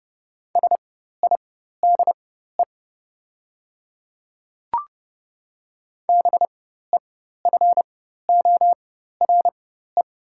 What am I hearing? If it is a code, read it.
Morse code, heard as a keyed tone, about 30 wpm, spaced as for 14 wpm, 700 Hz. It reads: SIDE BEFORE
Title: 4x Character Spacing - Top 300 Words - 30wpm